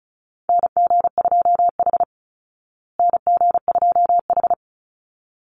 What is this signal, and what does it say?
Morse code, heard as a keyed tone, about 34 wpm, 700 Hz. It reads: NG2H NG2H
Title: Call Signs Repeated 3x - 35wpm